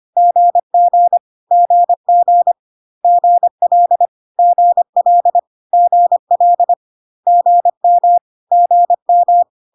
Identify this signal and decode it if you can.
Morse code, heard as a keyed tone, about 25 wpm, 700 Hz. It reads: GG GG GL GL GL GM GM
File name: CW abbreviations alphabetical repeated 3 times 25 25wpm